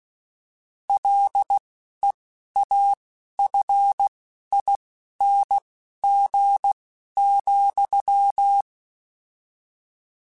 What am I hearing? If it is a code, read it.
Morse code, heard as a keyed tone, about 16 wpm, spaced as for 11 wpm, 800 Hz. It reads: LEAFING,